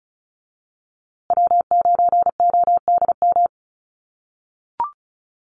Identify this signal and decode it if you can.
Morse code, heard as a keyed tone, about 34 wpm, 700 Hz. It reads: W9ODM
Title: USA Call Signs Repeated 3x 35wpm